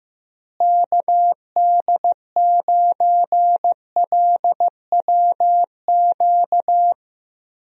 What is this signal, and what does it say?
Morse code, heard as a keyed tone, about 15 wpm, 700 Hz. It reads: KD9LWQ
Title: USA Call Signs - 6 Characters - Repeated 3x - 15wpm